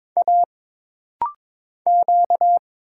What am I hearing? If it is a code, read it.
Morse code, heard as a keyed tone, about 22 wpm, 700 Hz. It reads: A Q
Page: This rapid-fire timing is geared to push your ICR (Instant Character Recognition) skills.